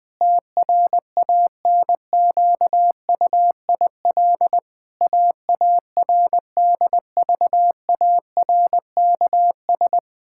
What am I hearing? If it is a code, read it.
Morse code, heard as a keyed tone, about 20 wpm, 700 Hz. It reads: TRANQUIL AARDVARKS
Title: Pangram Challenge v2 - 20wpm